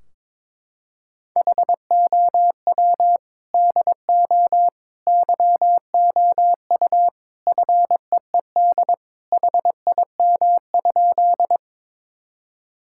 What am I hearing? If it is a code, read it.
Morse code, heard as a keyed tone, about 22 wpm, 700 Hz. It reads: HOW DO YOU FEED HIM?
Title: Sentences from Top 1600 Words 22wpm